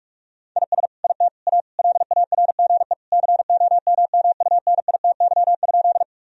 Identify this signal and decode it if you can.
Morse code, heard as a keyed tone, about 46 wpm, 700 Hz. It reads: IS IT A LARGE COMMUNITY?
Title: Sentences from Top 1000 Words - Review Part 4 - 45wpm